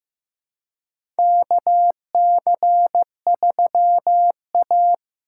Morse code, heard as a keyed tone, about 15 wpm, 700 Hz. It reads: KC3A
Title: USA Call Signs - 4 Characters - 3x Character Spacing - 15wpm